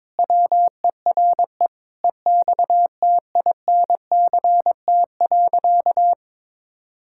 Morse code, heard as a keyed tone, about 22 wpm, 700 Hz. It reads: WERE EXTINCT.
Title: Pangram Challenge v2 - 22wpm